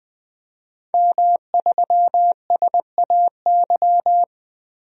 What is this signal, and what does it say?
Morse code, heard as a keyed tone, about 21 wpm, 700 Hz. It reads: M3SAY
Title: DXCC-100 Call Signs - 5 Characters - Repeated 3x - 20wpm